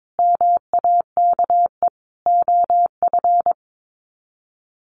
Morse code, heard as a keyed tone, about 22 wpm, 700 Hz. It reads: MAKE OF
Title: Sets of 5 Words 22wpm